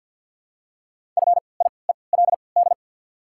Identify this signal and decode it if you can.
Morse code, heard as a keyed tone, about 53 wpm, spaced as for 29 wpm, 700 Hz. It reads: FIELD
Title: 3x Character Spacing - Top 500 Words - 50wpm